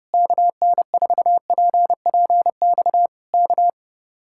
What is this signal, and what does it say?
Morse code, heard as a keyed tone, about 28 wpm, 700 Hz. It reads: KN4PPX K
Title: Speed-Racing Common QSO Phrases - 30wpm